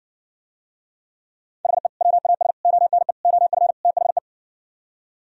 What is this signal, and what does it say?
Morse code, heard as a keyed tone, about 59 wpm, 700 Hz. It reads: HE WAS ONE OF THE